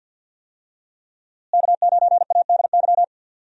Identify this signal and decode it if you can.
Morse code, heard as a keyed tone, about 52 wpm, 700 Hz. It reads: K9ADY
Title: USA Call Signs - 5 Characters - Repeated 3x - 50wpm